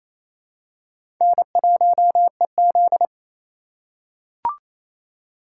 Morse code, heard as a keyed tone, about 28 wpm, 700 Hz. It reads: N1EZ
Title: USA Call Signs - 4x Character Spacing - 28wpm